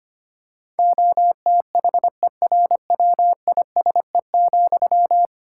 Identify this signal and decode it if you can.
Morse code, heard as a keyed tone, about 25 wpm, 700 Hz. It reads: OTHERWISE,
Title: Sentences from Top 1600 Words 25wpm